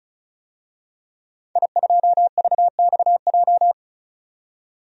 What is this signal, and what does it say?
Morse code, heard as a keyed tone, about 34 wpm, 700 Hz. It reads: I2VXJ
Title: DXCC-100 Call Signs - 5 Characters - 35wpm